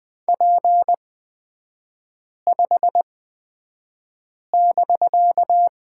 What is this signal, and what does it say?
Morse code, heard as a keyed tone, about 20 wpm, spaced as for 5 wpm, 700 Hz. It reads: P5<BK>
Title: 250 Lesson 40e - BK - Three Characters 20wpm